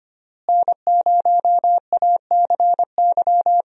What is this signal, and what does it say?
Morse code, heard as a keyed tone, about 25 wpm, 700 Hz. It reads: N0ACY
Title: USA Call Signs - 5 Characters - Repeated 2x - 25wpm